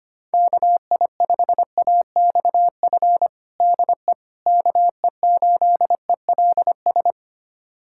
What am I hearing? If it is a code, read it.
Morse code, heard as a keyed tone, about 25 wpm, 700 Hz. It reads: KI5AXF DE KE8ELS
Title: USA Callsign DE Callsign - 6 Characters - 25wpm